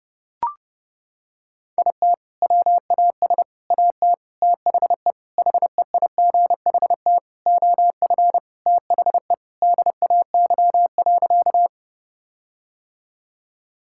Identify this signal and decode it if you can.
Morse code, heard as a keyed tone, about 30 wpm, 700 Hz. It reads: IT WAS AT THE HEIGHT OF THE DAY.
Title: Sentences from Top 1600 Words 30wpm